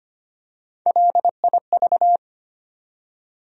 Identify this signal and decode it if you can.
Morse code, heard as a keyed tone, about 25 wpm, 700 Hz. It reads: LIV
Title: USA Common Names - Max 3 Letters - 25wpm